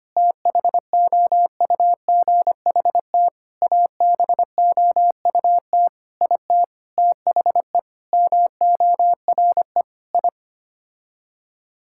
Morse code, heard as a keyed tone, about 25 wpm, 700 Hz. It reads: THOUGHT ABOUT IT THE MORE I
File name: Sets of 9 Words - 25wpm